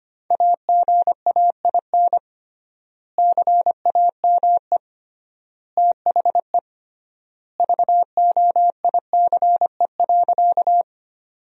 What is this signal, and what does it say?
Morse code, heard as a keyed tone, about 25 wpm, 700 Hz. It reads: AGAIN CAME THE VOICE.
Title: 3x Word-Spacing - Sentences from Top 500 Words - 25wpm